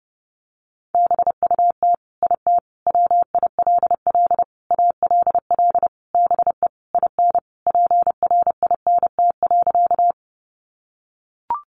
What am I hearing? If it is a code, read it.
Morse code, heard as a keyed tone, about 30 wpm, 700 Hz. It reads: BUT IT WILL ALL BE IN PRINT.
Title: Sentences from Top 1400 Words 30wpm